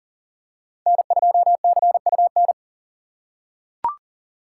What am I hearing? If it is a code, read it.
Morse code, heard as a keyed tone, about 38 wpm, 700 Hz. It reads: N2CUN
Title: USA Call Signs - 5 Characters - 3x Character Spacing - 40wpm